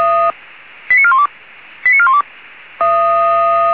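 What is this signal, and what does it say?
Five-tone selective call (ZVEI), heard as quick runs of five tones, tones of about 70 ms. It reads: 89412 89412
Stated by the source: ZVEI five-tone selective call 89418 received by SDR.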